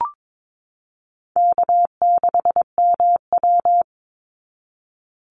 Morse code, heard as a keyed tone, about 22 wpm, 700 Hz. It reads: K6MW
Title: NA SOTA Chaser Callsigns 22wpm